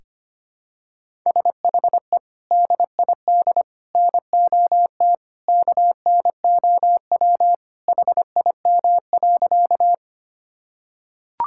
Morse code, heard as a keyed tone, about 25 wpm, 700 Hz. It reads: SHE DID NOT KNOW HIM.
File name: practice / 25wpm / Sentences with 3-Word Phrases - 25wpm